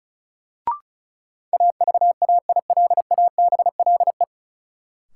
Morse code, heard as a keyed tone, about 36 wpm, 700 Hz. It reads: AVAILABLE
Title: Sentences from Top 600 Words - New Word Drill - 35wpm